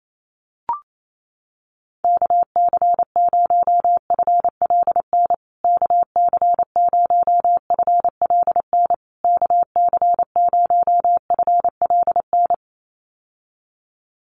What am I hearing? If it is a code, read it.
Morse code, heard as a keyed tone, about 28 wpm, 700 Hz. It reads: KC0FLN KC0FLN KC0FLN
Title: USA Call Signs Repeated 3x 28wpm